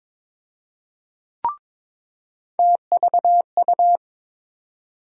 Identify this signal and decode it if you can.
Morse code, heard as a keyed tone, about 22 wpm, 700 Hz. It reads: TVU
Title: USA Call Sign Postfixes - 3 Letters 22wpm